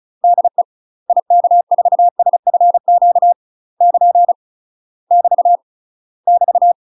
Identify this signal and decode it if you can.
Morse code, heard as a keyed tone, about 36 wpm, 700 Hz. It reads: DE IK4SFQ <KN> <BT> <BT>
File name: UK MASTER QSOs TRAINING / 35WPM / UK MASTER QSOs 0016